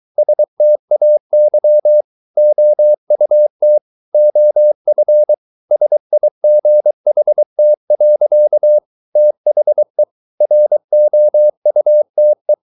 Morse code, heard as a keyed tone, about 23 wpm, 600 Hz. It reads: STAY OUT OF SIGHT. THE ROUTE